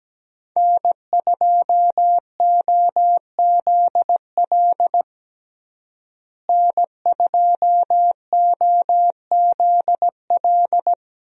Morse code, heard as a keyed tone, about 17 wpm, 700 Hz. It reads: N2OZL N2OZL
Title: USA Call Signs - 5 Characters - Repeated 3x - 17wpm